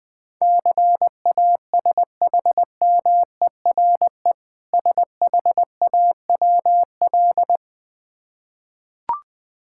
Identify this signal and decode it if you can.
Morse code, heard as a keyed tone, about 20 wpm, 700 Hz. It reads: CASHMERE SHAWL
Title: Difficult Words with Hints - 20wpm